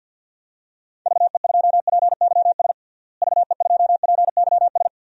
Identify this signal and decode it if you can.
Morse code, heard as a keyed tone, about 52 wpm, 700 Hz. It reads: VE2PYS VE2PYS
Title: DXCC-100 Call Signs - 6 Characters - Repeated 3x - 50wpm